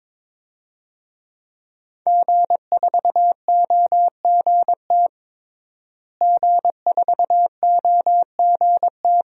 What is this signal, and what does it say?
Morse code, heard as a keyed tone, about 22 wpm, 700 Hz. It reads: G4OGT G4OGT
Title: DXCC-100 Call Signs - 5 Characters - Repeated 2x - 22wpm